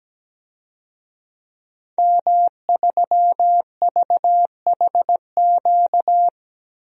Morse code, heard as a keyed tone, about 17 wpm, 700 Hz. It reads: M3VHQ
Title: DXCC-100 Call Signs - 5 Characters - Repeated 2x - 17wpm